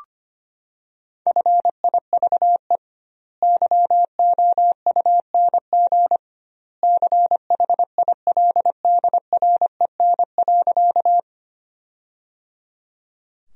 Morse code, heard as a keyed tone, about 25 wpm, 700 Hz. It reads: FIVE YOUNG CHILDREN.
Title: 2x Word-Spacing - Sentences from Top 400 Words - 25wpm